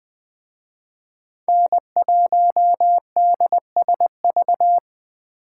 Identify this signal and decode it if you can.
Morse code, heard as a keyed tone, about 20 wpm, 700 Hz. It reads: N1DSV